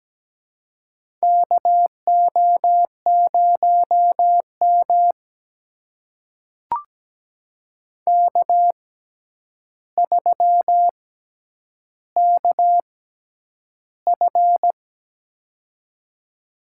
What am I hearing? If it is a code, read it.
Morse code, heard as a keyed tone, about 17 wpm, spaced as for 9 wpm, 700 Hz. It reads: KO0M K3KF
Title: USA Call Signs - 4 Characters - 6x Character Spacing - 17wpm